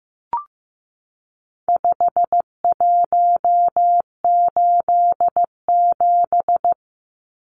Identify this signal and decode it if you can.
Morse code, heard as a keyed tone, about 15 wpm, 700 Hz. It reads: 5187
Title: Numbers - 4-Digit 15wpm